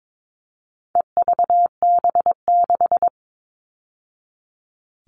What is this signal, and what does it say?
Morse code, heard as a keyed tone, about 22 wpm, 700 Hz. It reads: EVB6
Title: All Cut Numbers - 4-Digit 22wpm